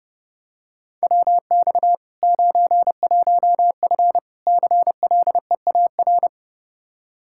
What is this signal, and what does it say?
Morse code, heard as a keyed tone, about 30 wpm, 700 Hz. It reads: WX 91F CLEAR
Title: K1USN SST QSO Practice 30wpm